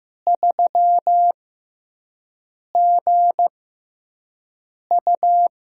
Morse code, heard as a keyed tone, about 15 wpm, spaced as for 5 wpm, 700 Hz. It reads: 3GU
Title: USA Call Signs - 4 Characters - 6x Character Spacing - 15wpm